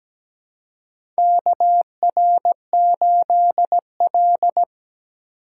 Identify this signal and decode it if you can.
Morse code, heard as a keyed tone, about 17 wpm, 700 Hz. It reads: KR8L
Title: K1USN SST QSO Practice - 17wpm